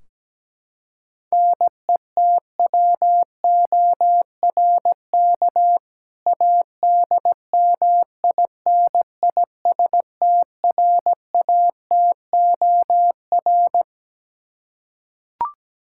Morse code, heard as a keyed tone, about 17 wpm, 700 Hz. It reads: NETWORK ADMINISTRATOR
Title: K1USN SST QSO Practice 17wpm